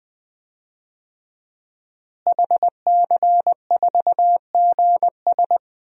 Morse code, heard as a keyed tone, about 21 wpm, 700 Hz. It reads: HC4GS
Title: DXCC-100 Call Signs - 5 Characters - Repeated 3x - 20wpm